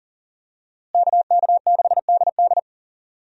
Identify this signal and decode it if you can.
Morse code, heard as a keyed tone, about 39 wpm, 700 Hz. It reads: KK6DD
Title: USA Call Signs - 5 Characters - Repeated 2x - 40wpm